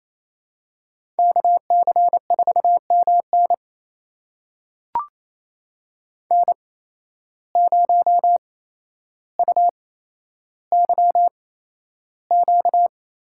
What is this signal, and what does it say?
Morse code, heard as a keyed tone, about 28 wpm, spaced as for 12 wpm, 700 Hz. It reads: KC4MN N0UYQ
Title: USA Call Signs - 5 Characters - 8x Character Spacing - 28wpm